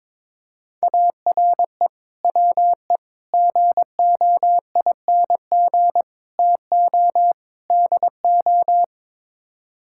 Morse code, heard as a keyed tone, about 22 wpm, 700 Hz. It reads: ARE WE GOING TO DO
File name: practice / 22wpm / Build A Phrase - 5-Word Phrases - 22wpm